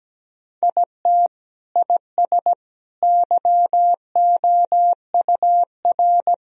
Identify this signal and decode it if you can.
Morse code, heard as a keyed tone, about 17 wpm, 700 Hz. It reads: IT IS YOUR
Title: Sentences from Top 1000 Words - Review Part 4 - 17wpm